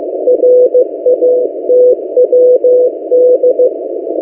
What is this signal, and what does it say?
Morse code, heard as a keyed tone, about 15 wpm, 500 Hz. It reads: RATWD
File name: M08a- -1355utc-8096khz